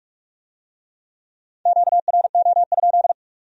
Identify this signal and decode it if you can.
Morse code, heard as a keyed tone, about 40 wpm, 700 Hz. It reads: QRO?